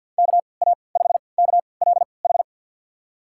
Morse code, heard as a keyed tone, about 50 wpm, spaced as for 28 wpm, 700 Hz. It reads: KA5KLH
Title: USA Call Signs - 6 Characters - 3x Character Spacing - 50wpm